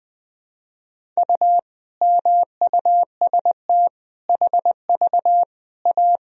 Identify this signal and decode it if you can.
Morse code, heard as a keyed tone, about 20 wpm, 700 Hz. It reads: U MUST HV A
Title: Sentences from Top 500 Words and Abbreviations 20wpm